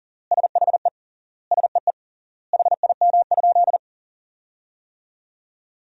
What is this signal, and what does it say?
Morse code, heard as a keyed tone, about 37 wpm, 700 Hz. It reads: SHE SEE HIM?
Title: 3x Word-Spacing - Sentences from Top 100 Words - 40wpm